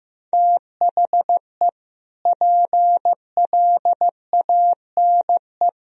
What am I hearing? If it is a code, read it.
Morse code, heard as a keyed tone, about 15 wpm, 700 Hz. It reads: THE PLANE
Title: Sentences from Top 1000 Words - Review Part 3 - 15wpm